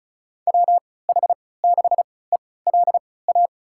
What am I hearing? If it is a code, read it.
Morse code, heard as a keyed tone, about 34 wpm, spaced as for 20 wpm, 700 Hz. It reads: WH6ELA
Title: USA Call Signs - 6 Characters - 3x Character Spacing - 35wpm